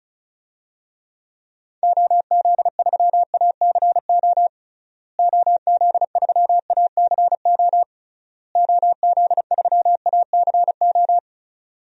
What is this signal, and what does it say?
Morse code, heard as a keyed tone, about 36 wpm, 700 Hz. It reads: OZ3ACO OZ3ACO OZ3ACO
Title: DXCC-100 Call Signs - 6 Characters - Repeated 3x - 35wpm